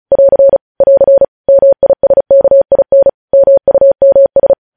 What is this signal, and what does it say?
Morse code, heard as a keyed tone, about 36 wpm, 550 Hz. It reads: <AR> <AR> MISKIN MUMS
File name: podcast / News headlines-1x-35x35